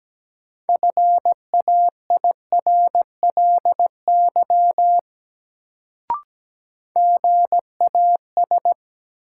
Morse code, heard as a keyed tone, about 17 wpm, 700 Hz. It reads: FAIRLY GAS
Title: Sentences from Top 1100 Words - New Word Drill Alt - 17wpm